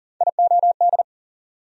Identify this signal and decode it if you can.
Morse code, heard as a keyed tone, about 40 wpm, 700 Hz. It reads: IOD
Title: USA Call Sign Postfixes - 3 Letters 40wpm